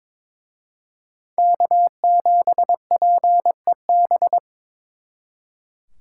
Morse code, heard as a keyed tone, about 26 wpm, 700 Hz. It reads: K7PEB
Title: Speed Racing - USA Call Signs 22wpm